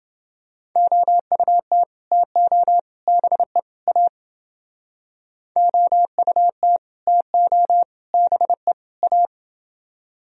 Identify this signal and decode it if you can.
Morse code, heard as a keyed tone, about 30 wpm, 700 Hz. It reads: OUT TO BE A OUT TO BE A